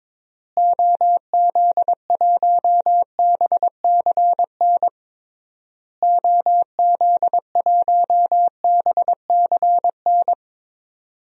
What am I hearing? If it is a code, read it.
Morse code, heard as a keyed tone, about 22 wpm, 700 Hz. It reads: OZ1BCN OZ1BCN
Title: DXCC-100 Call Signs - 6 Characters - Repeated 3x - 22wpm